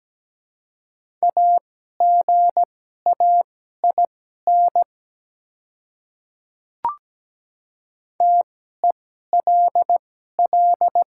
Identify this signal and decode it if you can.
Morse code, heard as a keyed tone, about 17 wpm, spaced as for 12 wpm, 700 Hz. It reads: AGAIN TELL
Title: 2x Character Spacing - Top 200 Words - 17wpm